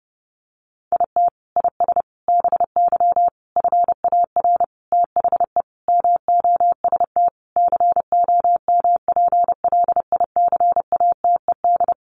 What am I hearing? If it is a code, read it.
Morse code, heard as a keyed tone, about 30 wpm, 700 Hz. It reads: IT IS BY FAR THE MOST COMPLICATED
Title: Sentences from Top 1200 Words 30wpm